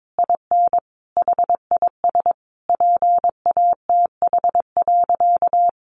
Morse code, heard as a keyed tone, about 22 wpm, 700 Hz. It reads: IN HIS PATH.
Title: Sentences from Top 1400 Words 22wpm